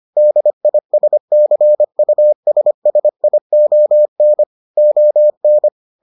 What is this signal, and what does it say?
Morse code, heard as a keyed tone, about 25 wpm, 600 Hz. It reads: DISCUSSION ON